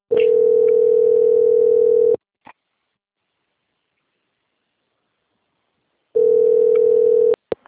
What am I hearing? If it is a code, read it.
Ringback tone (North America), 440 + 480 Hz.